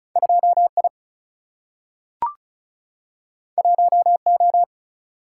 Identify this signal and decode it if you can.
Morse code, heard as a keyed tone, about 36 wpm, 700 Hz. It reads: 2I 1O
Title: Single-Letter Single-Number 35wpm